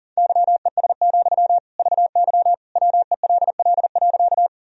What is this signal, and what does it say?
Morse code, heard as a keyed tone, about 40 wpm, 700 Hz. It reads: YES, VY WELL.
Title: Sentences from Top 500 Words and Abbreviations 40wpm